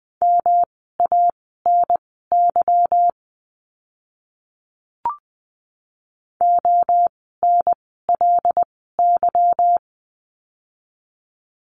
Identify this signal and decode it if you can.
Morse code, heard as a keyed tone, about 20 wpm, spaced as for 15 wpm, 700 Hz. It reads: MANY ONLY
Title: 237 Lesson 38f - Letter q - Words 20wpm